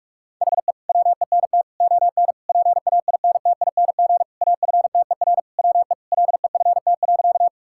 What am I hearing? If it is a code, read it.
Morse code, heard as a keyed tone, about 46 wpm, 700 Hz. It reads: HE WENT ON PAINTING AFTER WE LEFT.
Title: Sentences from Top 1000 Words - Review Part 6 - 45wpm